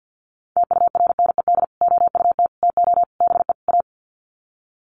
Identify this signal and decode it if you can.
Morse code, heard as a keyed tone, about 50 wpm, 700 Hz. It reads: TURNED OUT TO BE A
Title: Sets of 5 Words 50wpm